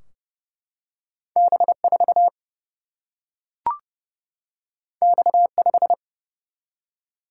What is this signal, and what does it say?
Morse code, heard as a keyed tone, about 30 wpm, 700 Hz. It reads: B4 X5
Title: Single-Letter Single-Number 30wpm